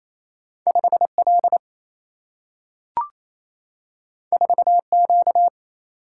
Morse code, heard as a keyed tone, about 28 wpm, 700 Hz. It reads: 5L 4Q
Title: Single-Letter Single-Number 28wpm